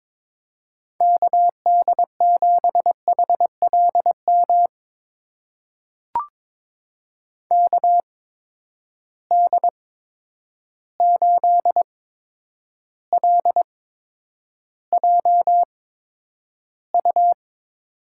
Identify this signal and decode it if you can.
Morse code, heard as a keyed tone, about 22 wpm, spaced as for 9 wpm, 700 Hz. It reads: KD7HLM KD8LJU
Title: USA Call Signs - 8x Character Spacing - 22wpm